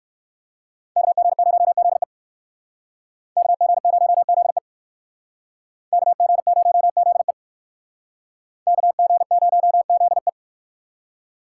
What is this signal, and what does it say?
Morse code, heard as a keyed tone, about 57 wpm, 700 Hz. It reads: KG0ZE KG0ZE KG0ZE KG0ZE